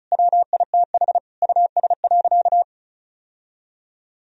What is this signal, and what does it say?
Morse code, heard as a keyed tone, about 34 wpm, 700 Hz. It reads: WITH US.
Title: Sentences from Top 1000 Words - Review Part 4 - 35wpm